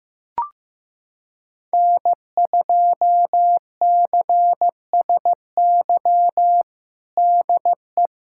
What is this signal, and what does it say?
Morse code, heard as a keyed tone, about 15 wpm, 700 Hz. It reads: N2CSY DE
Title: USA Callsign DE Callsign - 5 Characters - 15wpm